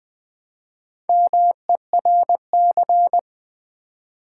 Morse code, heard as a keyed tone, about 20 wpm, 700 Hz. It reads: MERC
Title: Build A Word - 1 Letter At A Time - 5-Letter Words - 20wpm